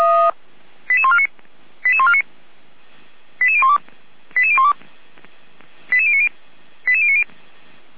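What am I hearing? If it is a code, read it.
Five-tone selective call (ZVEI), heard as quick runs of five tones, tones of about 70 ms. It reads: 88149 88149 88012 88012 88090 88090
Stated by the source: ZVEI five-tone selective call 88397 received by SDR.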